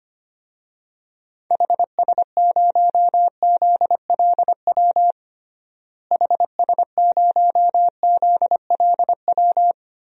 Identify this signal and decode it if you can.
Morse code, heard as a keyed tone, about 25 wpm, 700 Hz. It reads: HS0ZLW HS0ZLW
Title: DXCC-100 Call Signs - 6 Characters - Repeated 2x - 25wpm